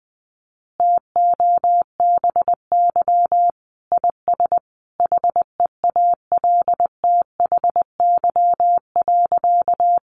Morse code, heard as a keyed tone, about 19 wpm, 700 Hz. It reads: TOBY IS HEALTHY.
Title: Easy Phrases 20wpm